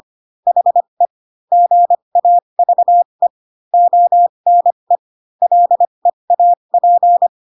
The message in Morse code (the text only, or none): HE GAVE ONE LEAP